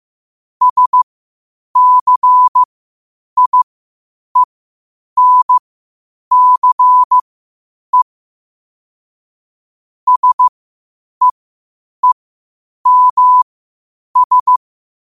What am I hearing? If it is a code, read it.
Morse code, heard as a keyed tone, about 15 wpm, spaced as for 8 wpm, 1000 Hz. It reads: SCIENCE SEEMS
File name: qotdfiles / 8wpm